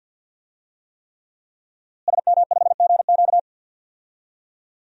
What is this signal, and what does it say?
Morse code, heard as a keyed tone, about 51 wpm, 700 Hz. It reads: SM5GQ
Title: DXCC-100 Call Signs - 5 Characters - Repeated 3x - 50wpm